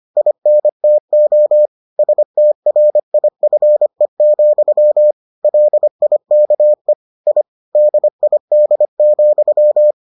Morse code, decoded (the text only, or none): INTO STRIFE, LIKE I DID,